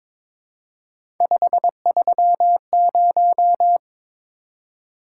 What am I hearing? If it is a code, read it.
Morse code, heard as a keyed tone, about 22 wpm, 700 Hz. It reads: 530
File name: practice / 22wpm / Three Letter-Number Combinations - 22wpm